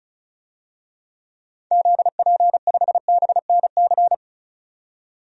Morse code, heard as a keyed tone, about 36 wpm, 700 Hz. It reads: ZP5BNC
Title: DXCC-100 Call Signs - 6 Characters - 35wpm